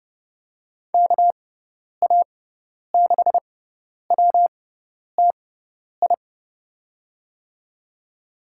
Morse code, heard as a keyed tone, about 30 wpm, spaced as for 10 wpm, 700 Hz. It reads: KA6WTI
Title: USA Call Signs - 6x Character Spacing - 30wpm